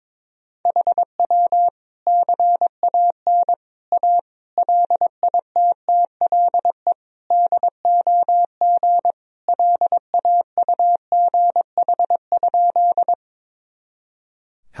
Morse code, heard as a keyed tone, about 22 wpm, 700 Hz. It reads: HW CAN A LITTLE DOG LAUGH?
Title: Sentences from Top 500 Words and Abbreviations 22wpm